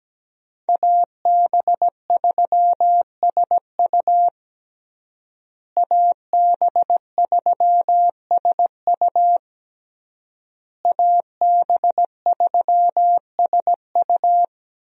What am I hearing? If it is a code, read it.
Morse code, heard as a keyed tone, about 17 wpm, 700 Hz. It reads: AB3SU AB3SU AB3SU
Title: USA Call Signs - 5 Characters - Repeated 3x - 17wpm